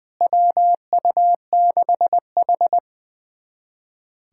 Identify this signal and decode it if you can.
Morse code, heard as a keyed tone, about 20 wpm, 700 Hz. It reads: WU6H
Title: USA Callsign DE Callsign - 4 Characters - 20wpm